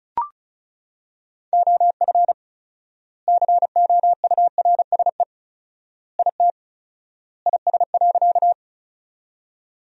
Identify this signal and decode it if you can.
Morse code, heard as a keyed tone, about 36 wpm, 700 Hz. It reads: OF COURSE IT IS.
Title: 4x Word-Spacing - Sentences from Top 500 Words - 35wpm